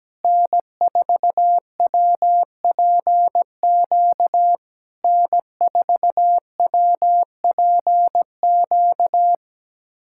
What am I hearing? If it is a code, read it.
Morse code, heard as a keyed tone, about 17 wpm, 700 Hz. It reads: N4WPQ N4WPQ
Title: SOTA QSO Practice 17wpm